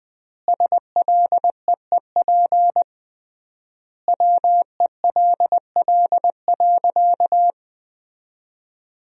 Morse code, heard as a keyed tone, about 20 wpm, 700 Hz. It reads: SLEEP WELL.
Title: 3x Word-Spacing - Sentences from Top 500 Words - 20wpm